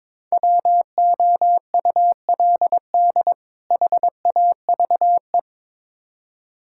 Morse code, heard as a keyed tone, about 21 wpm, 700 Hz. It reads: WOULD HAVE
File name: Build A Phrase - 2-Word Phrases - 22wpm